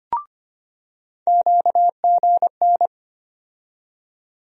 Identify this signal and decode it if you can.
Morse code, heard as a keyed tone, about 25 wpm, 700 Hz. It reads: QGN
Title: USA Call Sign Postfixes - 3 Letters 25wpm